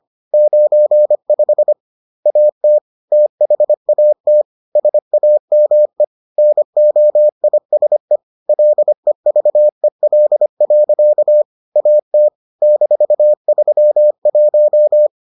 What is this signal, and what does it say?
Morse code, heard as a keyed tone, about 25 wpm, 600 Hz. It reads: 95 AT THAT SAME NOISE LEVEL. AT -31